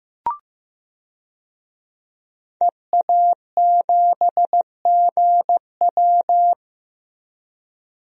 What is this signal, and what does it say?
Morse code, heard as a keyed tone, about 15 wpm, 700 Hz. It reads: EA7GW
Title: DXCC-100 Call Signs - 5 Characters - Repeated 2x - 15wpm